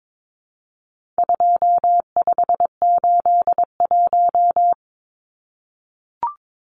Morse code, heard as a keyed tone, about 22 wpm, 700 Hz. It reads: 2581
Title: Numbers - 4-Digit 22wpm